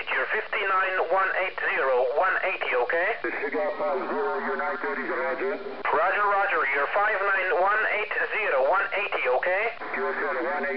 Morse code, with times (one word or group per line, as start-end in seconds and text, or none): none